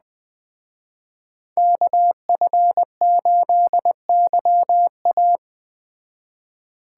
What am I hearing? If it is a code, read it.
Morse code, heard as a keyed tone, about 20 wpm, 700 Hz. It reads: KF8YA